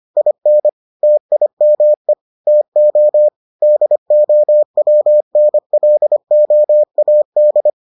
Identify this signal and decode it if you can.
Morse code, heard as a keyed tone, about 25 wpm, 600 Hz. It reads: IN TIME TO DOWNLOAD